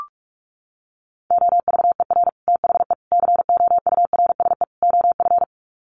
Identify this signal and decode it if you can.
Morse code, heard as a keyed tone, about 44 wpm, 700 Hz. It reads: OVER THE COURSE OF
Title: Sets of 4 Words 45wpm